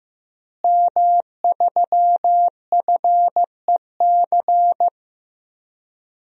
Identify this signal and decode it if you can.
Morse code, heard as a keyed tone, about 15 wpm, 700 Hz. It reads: M3FEC
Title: DXCC-100 Call Signs - 5 Characters - Repeated 2x - 15wpm